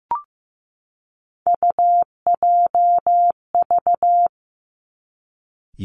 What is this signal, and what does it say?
Morse code, heard as a keyed tone, about 15 wpm, 700 Hz. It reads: UJV
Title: Three Challenging Letter-Number Combinations 15wpm